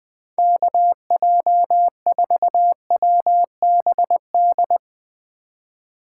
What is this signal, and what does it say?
Morse code, heard as a keyed tone, about 20 wpm, 700 Hz. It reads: KJ4WBD
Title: Speed Racing - USA 2x3 Call Signs 17wpm